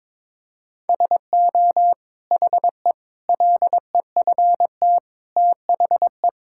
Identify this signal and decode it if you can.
Morse code, heard as a keyed tone, about 22 wpm, 700 Hz. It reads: SO HE LEFT THE
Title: Sentences from Top 1600 Words 22wpm